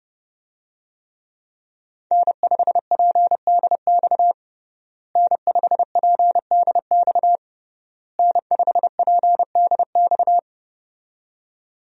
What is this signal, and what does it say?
Morse code, heard as a keyed tone, about 32 wpm, 700 Hz. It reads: N5PDX N5PDX N5PDX
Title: USA Call Signs - 5 Characters - Repeated 3x - 30wpm